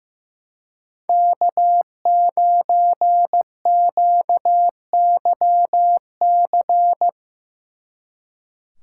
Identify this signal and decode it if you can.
Morse code, heard as a keyed tone, about 16 wpm, 700 Hz. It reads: K9QYC
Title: Speed Racing - USA 1x3 Call Signs 15wpm